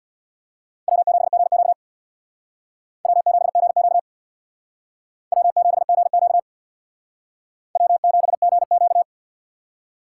Morse code, heard as a keyed tone, about 61 wpm, 700 Hz. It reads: W7GQ W7GQ W7GQ W7GQ
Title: Speed Racing - USA Call Signs 50wpm